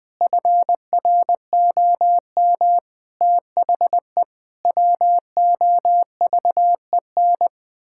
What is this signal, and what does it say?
Morse code, heard as a keyed tone, about 20 wpm, 700 Hz. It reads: FROM THE WOVEN
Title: Pangram Challenge v2 - 20wpm